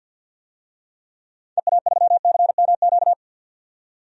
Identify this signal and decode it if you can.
Morse code, heard as a keyed tone, about 49 wpm, 700 Hz. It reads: EA3CMQ
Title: DXCC-100 Call Signs - 6 Characters - 50wpm